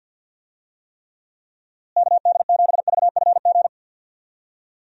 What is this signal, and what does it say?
Morse code, heard as a keyed tone, about 48 wpm, 700 Hz. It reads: KD7VFG
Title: USA Call Signs - 6 Characters - Repeated 2x - 50wpm